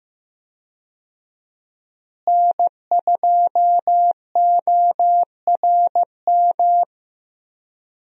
Morse code, heard as a keyed tone, about 15 wpm, 700 Hz. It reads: N2ORM
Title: USA Call Signs - 5 Characters - Repeated 3x - 15wpm